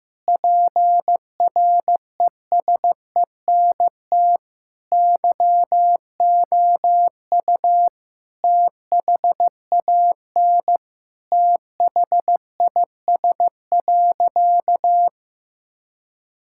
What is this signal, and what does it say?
Morse code, heard as a keyed tone, about 14 wpm, 700 Hz. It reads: PRESENT YOU THAN THIS.
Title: Sentences from Top 1000 Words - Review Part 6 - 15wpm